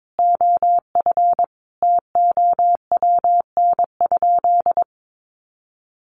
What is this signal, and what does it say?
Morse code, heard as a keyed tone, about 22 wpm, 700 Hz. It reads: OF TOWN?
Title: Sentences from Top 1300 Words 22wpm